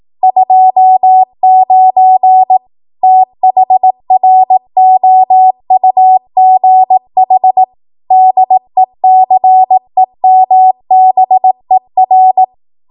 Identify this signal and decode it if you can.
Morse code, heard as a keyed tone, about 18 wpm, 750 Hz. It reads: 29 THROUGH DECEMBER
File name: Archive / 18 WPM / 18WPM